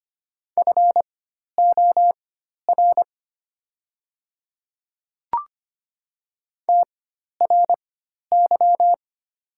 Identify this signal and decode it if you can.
Morse code, heard as a keyed tone, about 25 wpm, spaced as for 12 wpm, 700 Hz. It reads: FOR TRY
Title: 4x Character Spacing - Top 300 Words - 25wpm